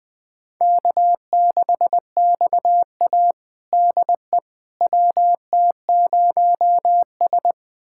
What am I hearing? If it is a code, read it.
Morse code, heard as a keyed tone, about 20 wpm, 700 Hz. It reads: K6XA DE WT0S
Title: USA Callsign DE Callsign - 4 Characters - 20wpm